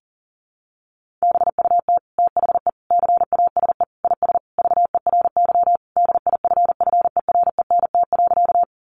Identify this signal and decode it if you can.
Morse code, heard as a keyed tone, about 40 wpm, 700 Hz. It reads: BUT THE CASE IS VERY DIFFERENT.
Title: Sentences from Top 700 Words 40wpm